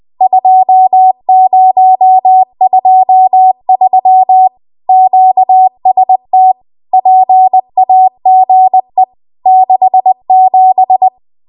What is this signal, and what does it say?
Morse code, heard as a keyed tone, about 20 wpm, 750 Hz. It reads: 2023 QST PAGE 67